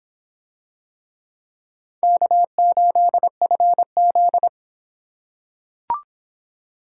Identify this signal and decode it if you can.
Morse code, heard as a keyed tone, about 26 wpm, 700 Hz. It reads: K8FZ